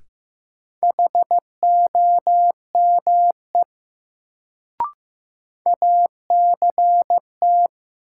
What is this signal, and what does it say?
Morse code, heard as a keyed tone, about 15 wpm, 700 Hz. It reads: HOME ACT
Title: Sentences from Top 200 Words - New Word Drill - 15wpm